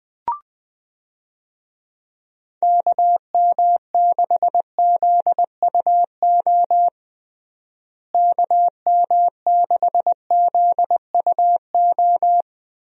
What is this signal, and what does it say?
Morse code, heard as a keyed tone, about 20 wpm, 700 Hz. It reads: KM6ZUO KM6ZUO
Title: USA Call Signs - 6 Characters - Repeated 2x - 20wpm